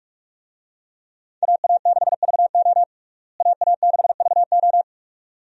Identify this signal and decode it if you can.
Morse code, heard as a keyed tone, about 46 wpm, 700 Hz. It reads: AA6VO AA6VO
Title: USA Call Signs - 5 Characters - Repeated 2x - 45wpm